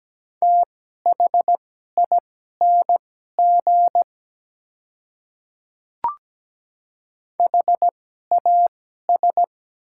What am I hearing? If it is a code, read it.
Morse code, heard as a keyed tone, about 17 wpm, spaced as for 12 wpm, 700 Hz. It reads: THING HAS
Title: 2x Character Spacing - Top 100 Words - 17wpm